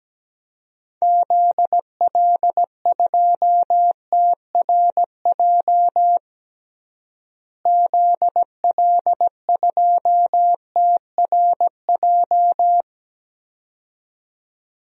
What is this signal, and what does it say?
Morse code, heard as a keyed tone, about 17 wpm, 700 Hz. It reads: ZL2TRJ ZL2TRJ
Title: DXCC-100 Call Signs - 6 Characters - Repeated 2x - 17wpm